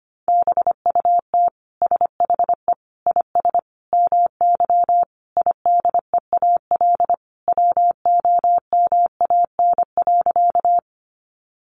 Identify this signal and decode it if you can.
Morse code, heard as a keyed tone, about 25 wpm, 700 Hz. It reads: BUT SHE IS MY IDEAL WOMAN.
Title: Sentences from Top 1400 Words 25wpm